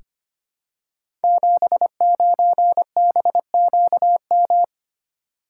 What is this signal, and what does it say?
Morse code, heard as a keyed tone, about 25 wpm, 700 Hz. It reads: 79BQM
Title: Code Groups - 5 Letter:Numbers - 25wpm